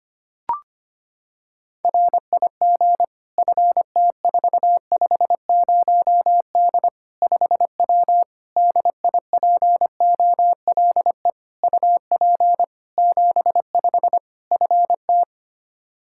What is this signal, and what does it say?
Morse code, heard as a keyed tone, about 25 wpm, 700 Hz. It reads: RIG FT450D 5W DIPOLE UP 75 FT
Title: Practice QSO - Exchange 2 - Info - 25wpm